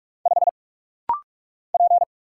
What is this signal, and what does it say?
Morse code, heard as a keyed tone, about 46 wpm, 700 Hz. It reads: 5 P
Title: Single Letter-Number - Rapid-Fire 45wpm